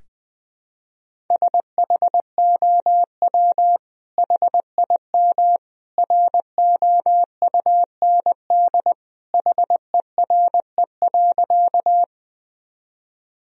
Morse code, heard as a keyed tone, about 21 wpm, 700 Hz. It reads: SHOW HIM ROUND HERE.
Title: Sentences from Top 1000 Words - Review Part 1 - 20wpm